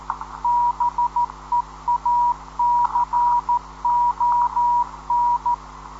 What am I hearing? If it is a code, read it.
Morse code, heard as a keyed tone, about 13 wpm, 960 Hz. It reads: BEACON
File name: 10.133kHz